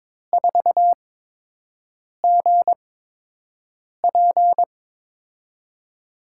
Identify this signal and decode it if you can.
Morse code, heard as a keyed tone, about 22 wpm, spaced as for 6 wpm, 700 Hz. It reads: 4GP
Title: USA Call Signs - 5 Characters - 8x Character Spacing - 22wpm